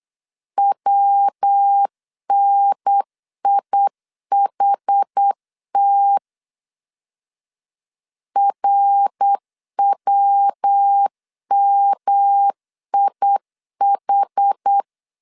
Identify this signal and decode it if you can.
Morse code, heard as a keyed tone, about 8 wpm, 800 Hz. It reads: WNIHT RWMIH
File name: CW / DET / 1 3 1500C S B 8wpm 4141